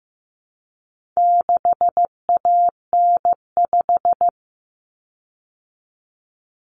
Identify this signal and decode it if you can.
Morse code, heard as a keyed tone, about 15 wpm, 700 Hz. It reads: 6AN5
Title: Common Cut Numbers - 4-Digit 15wpm